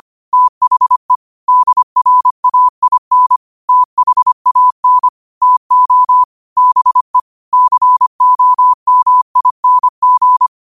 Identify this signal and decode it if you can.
Morse code, heard as a keyed tone, about 25 wpm, 1000 Hz. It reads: THE DRAIN THAN TO BE COMING